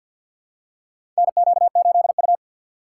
Morse code, heard as a keyed tone, about 53 wpm, 700 Hz. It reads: NQ8U
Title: USA Call Signs - 6x Character Spacing - 50wpm